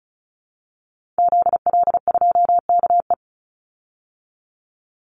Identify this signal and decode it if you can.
Morse code, heard as a keyed tone, about 34 wpm, 700 Hz. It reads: ZL2KE
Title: Call Signs Repeated 2x - 35wpm